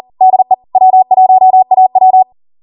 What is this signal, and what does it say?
Morse code, heard as a keyed tone, about 40 wpm, 750 Hz. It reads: DE W1AW